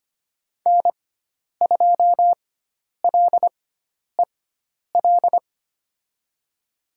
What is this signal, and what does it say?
Morse code, heard as a keyed tone, about 25 wpm, spaced as for 10 wpm, 700 Hz. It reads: N2LEL
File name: Lesson 16g - Letter c - Call Signs 25wpm